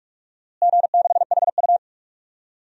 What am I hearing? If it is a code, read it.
Morse code, heard as a keyed tone, about 44 wpm, 690 Hz. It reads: G6HU
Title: DXCC-100 Call Signs - 4 Characters - Repeated 3x - 45wpm